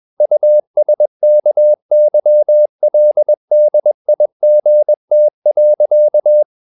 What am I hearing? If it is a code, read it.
Morse code, heard as a keyed tone, about 21 wpm, 600 Hz. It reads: USKYLDIGT.
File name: Ferie QRS 21wpm